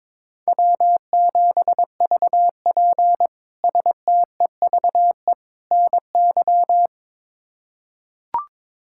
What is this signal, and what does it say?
Morse code, heard as a keyed tone, about 22 wpm, 700 Hz. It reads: W7VP STEVE NY
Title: K1USN SST QSO Practice - 22wpm